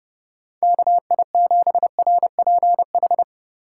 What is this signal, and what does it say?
Morse code, heard as a keyed tone, about 30 wpm, 700 Hz. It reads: KI7RPH
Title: USA Call Signs - 6 Characters - 8x Character Spacing - 30wpm